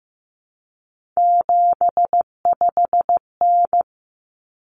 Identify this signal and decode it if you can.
Morse code, heard as a keyed tone, about 15 wpm, 700 Hz. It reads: 75N
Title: Common Cut Numbers - 3-Digit 15wpm